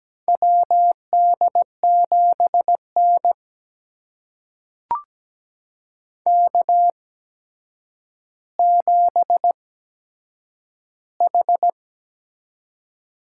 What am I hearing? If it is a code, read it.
Morse code, heard as a keyed tone, about 17 wpm, spaced as for 8 wpm, 700 Hz. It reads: WD7N K7H
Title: USA Call Signs - 4 Characters - 8x Character Spacing - 17wpm